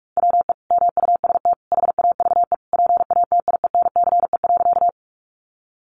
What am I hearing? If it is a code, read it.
Morse code, heard as a keyed tone, about 44 wpm, 700 Hz. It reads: WE MUST HAVE PATIENCE.
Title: Sentences from Top 1300 Words 45wpm